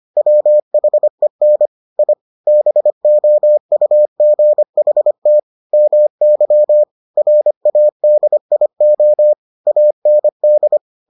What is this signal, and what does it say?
Morse code, heard as a keyed tone, about 25 wpm, 600 Hz. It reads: WHEN I BOUGHT MY RADIO AND